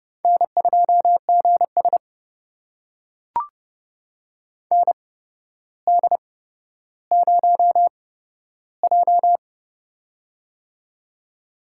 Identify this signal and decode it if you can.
Morse code, heard as a keyed tone, about 30 wpm, spaced as for 13 wpm, 700 Hz. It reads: N2GS ND0J
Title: USA Call Signs - 4 Characters - 8x Character Spacing - 30wpm